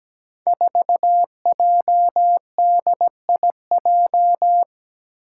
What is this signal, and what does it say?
Morse code, heard as a keyed tone, about 17 wpm, 700 Hz. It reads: 4JDIJ
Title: Five Challenging Letter-Number Combinations 17wpm